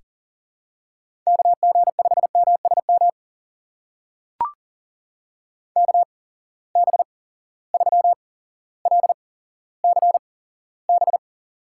Morse code, heard as a keyed tone, about 38 wpm, spaced as for 17 wpm, 700 Hz. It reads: KG5MSM KB3LCB
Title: USA Call Signs - 6 Characters - 8x Character Spacing - 40wpm